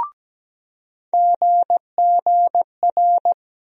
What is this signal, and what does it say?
Morse code, heard as a keyed tone, about 17 wpm, 700 Hz. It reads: GGR
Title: USA Call Sign Postfixes - 3 Letters 17wpm